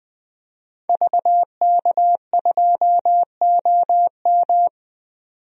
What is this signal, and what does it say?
Morse code, heard as a keyed tone, about 21 wpm, 700 Hz. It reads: VK2OM
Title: DXCC-100 Call Signs - 5 Characters - 20wpm